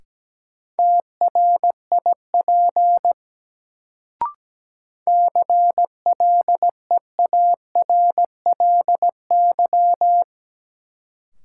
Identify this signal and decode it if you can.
Morse code, heard as a keyed tone, about 17 wpm, 700 Hz. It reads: TRIP CLEARLY
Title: Sentences from Top 1300 Words - New Word Drill Alt - 17wpm